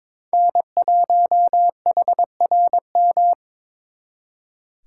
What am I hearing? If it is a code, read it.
Morse code, heard as a keyed tone, about 26 wpm, 700 Hz. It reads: N1HRM
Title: Speed Racing - USA 1x3 Call Signs 22wpm